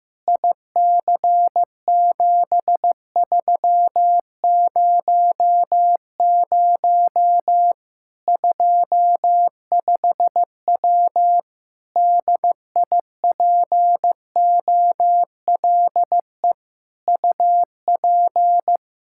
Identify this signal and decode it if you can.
Morse code, heard as a keyed tone, about 15 wpm, 700 Hz. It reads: IC7300 25W DIPOLE UP
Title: K1USN SST QSO Practice 15wpm